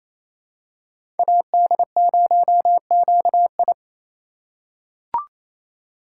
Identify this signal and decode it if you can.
Morse code, heard as a keyed tone, about 28 wpm, 700 Hz. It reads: AD0QI
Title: USA Call Signs - 5 Characters - 2x Character Spacing - 28wpm